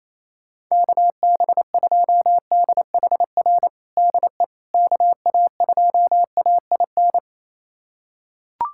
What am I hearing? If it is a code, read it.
Morse code, heard as a keyed tone, about 28 wpm, 700 Hz. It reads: KB2DHR DE KA2AIN
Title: USA Callsign DE Callsign - 6 Characters - 28wpm